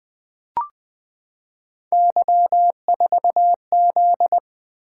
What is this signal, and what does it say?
Morse code, heard as a keyed tone, about 20 wpm, 700 Hz. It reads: Y4Z
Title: Three Letter-Number Combinations - 20wpm